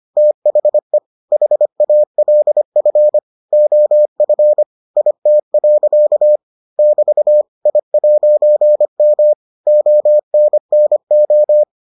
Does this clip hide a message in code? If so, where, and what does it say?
Yes: Morse code from 0.2 to 11.6 s, heard as a keyed tone, about 25 wpm, 600 Hz. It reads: THE HALF OF IT. <BT> I'M ONNO